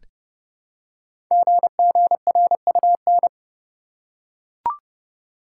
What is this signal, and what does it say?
Morse code, heard as a keyed tone, about 30 wpm, 700 Hz. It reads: GGRUN
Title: Code Groups - 5 Letters - 30wpm